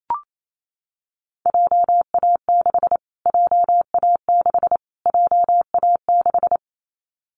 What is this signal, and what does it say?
Morse code, heard as a keyed tone, about 28 wpm, 700 Hz. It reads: JA6 JA6 JA6
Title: Callsign Prefixes for Top 25 DXCC - Repeated 3x - 28wpm